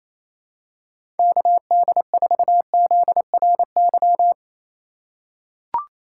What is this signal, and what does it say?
Morse code, heard as a keyed tone, about 28 wpm, 700 Hz. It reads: KD4ZRY
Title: USA Call Signs - 6 Characters - 6x Character Spacing - 28wpm